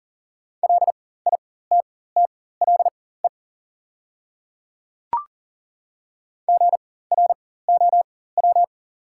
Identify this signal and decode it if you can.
Morse code, heard as a keyed tone, about 40 wpm, spaced as for 19 wpm, 690 Hz. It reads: LITTLE GROW
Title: 4x Character Spacing - Top 300 Words - 40wpm